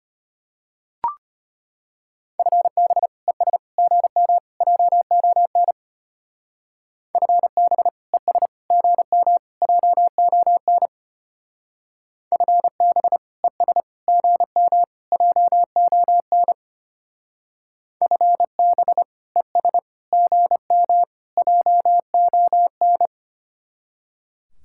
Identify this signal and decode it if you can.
Morse code, heard as a keyed tone, about 32 wpm, 700 Hz. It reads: FB ES GM JON FB ES GM JON FB ES GM JON FB ES GM JON